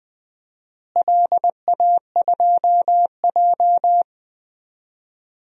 Morse code, heard as a keyed tone, about 20 wpm, 700 Hz. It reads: LA2J
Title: DXCC-100 Call Signs - 4 Characters - Repeated 2x - 20wpm